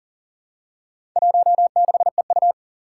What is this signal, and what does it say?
Morse code, heard as a keyed tone, about 40 wpm, 700 Hz. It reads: 16EU
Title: Four Challenging Letter-Number Combinations 40wpm